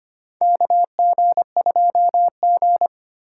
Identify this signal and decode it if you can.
Morse code, heard as a keyed tone, about 25 wpm, 700 Hz. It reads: KG2G
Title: USA Call Signs - 4 Characters - 2x Character Spacing - 25wpm